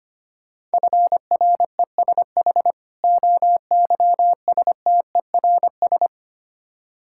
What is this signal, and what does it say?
Morse code, heard as a keyed tone, about 25 wpm, 690 Hz. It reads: FRESH OYSTERS
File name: practice / 25wpm / Difficult Words with Hints - 25wpm